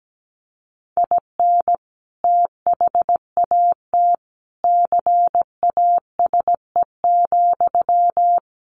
Morse code, heard as a keyed tone, about 17 wpm, 700 Hz. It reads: IN THAT CASE,
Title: Sentences from Top 1400 Words 17wpm